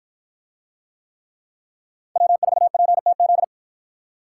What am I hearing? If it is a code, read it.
Morse code, heard as a keyed tone, about 53 wpm, 700 Hz. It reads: W4PTZ